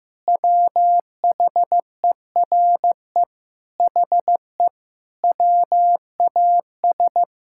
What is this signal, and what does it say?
Morse code, heard as a keyed tone, about 16 wpm, 700 Hz. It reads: WHERE HE WAS
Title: Sentences from Top 1600 Words 15wpm